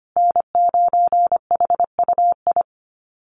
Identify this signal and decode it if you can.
Morse code, heard as a keyed tone, about 25 wpm, 700 Hz. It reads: N9HUI
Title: USA Call Signs Repeated 3x 25wpm